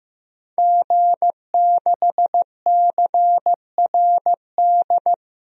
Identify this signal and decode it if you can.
Morse code, heard as a keyed tone, about 15 wpm, 700 Hz. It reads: G6CRD
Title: DXCC-100 Call Signs - 5 Characters - 15wpm